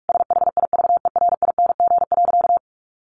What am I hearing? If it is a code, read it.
Morse code, heard as a keyed tone, about 44 wpm, 700 Hz. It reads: SHIVERING.
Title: A Sound Of Thunder 45wpm